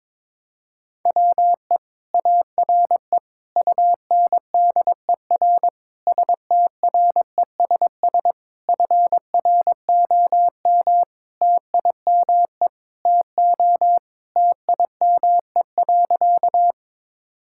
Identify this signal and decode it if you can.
Morse code, heard as a keyed tone, about 22 wpm, 700 Hz. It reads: WE ARE UNDER STRESS FROM TIME TO TIME.
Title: Sentences from Top 1000 Words - Review Part 6 - 22wpm